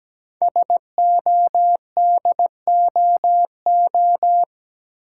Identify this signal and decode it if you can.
Morse code, heard as a keyed tone, about 17 wpm, 700 Hz. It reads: SODOO
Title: Code Groups - 5 Letters - 17wpm